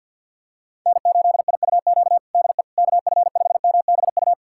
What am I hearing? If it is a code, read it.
Morse code, heard as a keyed tone, about 50 wpm, 700 Hz. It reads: N8IUQ DE KF5MBU
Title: USA Callsign DE Callsign 50wpm